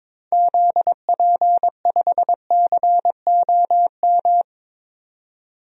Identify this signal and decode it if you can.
Morse code, heard as a keyed tone, about 22 wpm, 700 Hz. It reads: ZP5COM
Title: DXCC-100 Call Signs - 6 Characters - 22wpm